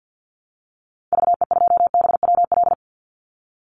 Morse code, heard as a keyed tone, about 50 wpm, 700 Hz. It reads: VE2BWL
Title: Call Signs Repeated 2x - 50wpm